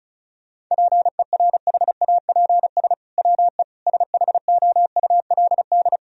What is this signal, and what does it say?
Morse code, heard as a keyed tone, about 34 wpm, 700 Hz. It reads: PERHAPS WE SHOULD
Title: Sentences from Top 1000 Words - Review Part 4 - 35wpm